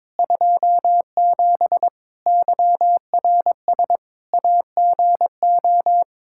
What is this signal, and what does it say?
Morse code, heard as a keyed tone, about 21 wpm, 700 Hz. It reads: 27 YRS AGO
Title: Practice QSO - Exchange 3 - More Info - 22wpm